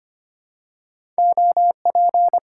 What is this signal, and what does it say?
Morse code, heard as a keyed tone, about 25 wpm, 700 Hz. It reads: OP
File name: Top 90 QSO Elements - 8x Character Spacing - 25wpm